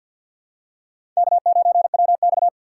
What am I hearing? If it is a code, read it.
Morse code, heard as a keyed tone, about 53 wpm, 700 Hz. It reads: K9WX
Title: USA Call Signs - 4 Characters - 2x Character Spacing - 50wpm